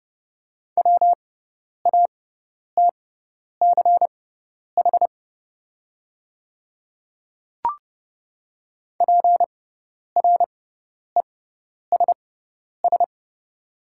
Morse code, heard as a keyed tone, about 30 wpm, spaced as for 10 wpm, 690 Hz. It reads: WATCH PRESS
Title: 6x Character Spacing - Top 500 Words - 30wpm